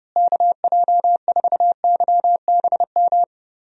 Morse code, heard as a keyed tone, about 30 wpm, 700 Hz. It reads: KJ4YBM
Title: USA Call Signs - 6 Characters - Repeated 2x - 30wpm